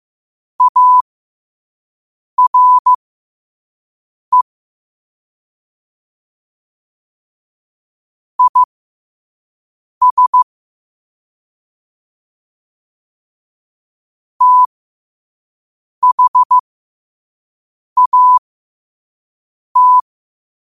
Morse code, heard as a keyed tone, about 15 wpm, spaced as for 5 wpm, 1000 Hz. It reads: ARE IS THAT